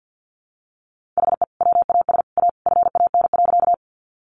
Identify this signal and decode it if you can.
Morse code, heard as a keyed tone, about 50 wpm, 700 Hz. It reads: HE WAS A FAN.
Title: Sentences from Top 1300 Words 50wpm